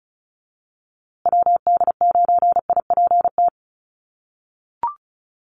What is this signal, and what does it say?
Morse code, heard as a keyed tone, about 34 wpm, 700 Hz. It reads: WD9IPT
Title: USA Call Signs Repeated 3x 35wpm